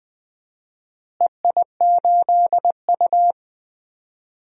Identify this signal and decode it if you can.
Morse code, heard as a keyed tone, about 20 wpm, 700 Hz. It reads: EI8U
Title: DXCC-100 Call Signs - 4 Characters - Repeated 2x - 20wpm